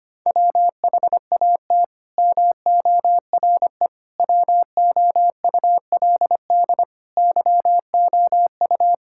Morse code, heard as a keyed tone, about 25 wpm, 700 Hz. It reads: WHAT MORE WOULD YOU
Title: Sentences from Top 1000 Words - Review Part 6 - 25wpm